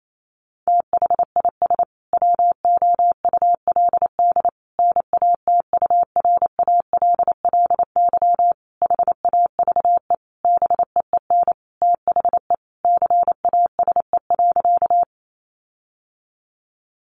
Morse code, heard as a keyed tone, about 28 wpm, 700 Hz. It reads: THIS WOULD NATURALLY HAVE BEEN THE CASE.
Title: Sentences from Top 1300 Words 28wpm